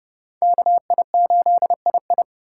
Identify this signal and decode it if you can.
Morse code, heard as a keyed tone, about 30 wpm, 700 Hz. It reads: KI8II
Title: USA Call Signs - 5 Characters - 4x Character Spacing - 30wpm